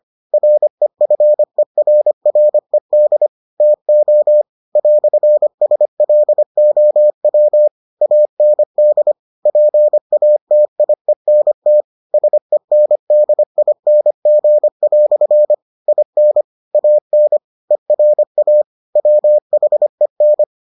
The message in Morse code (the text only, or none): REFERRED TO "SLOW AND PATIENT SENDING" IN AN ERA WHEN